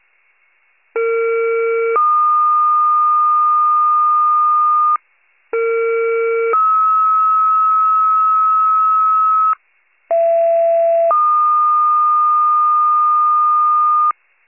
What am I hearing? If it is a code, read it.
Two-tone page, heard as a one-second tone, then a three-second tone, about 460 then 1220 Hz, about 460 then 1290 Hz, about 670 then 1220 Hz. It is heard from 0.9 to 14.1 s.
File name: wcalert